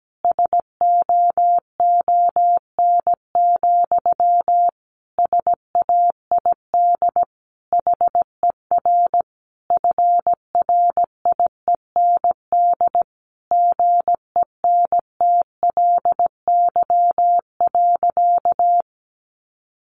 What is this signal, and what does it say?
Morse code, heard as a keyed tone, about 17 wpm, 700 Hz. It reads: SOON, SAID HER FRIEND GENTLY.
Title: Sentences from Top 1500 Words 17wpm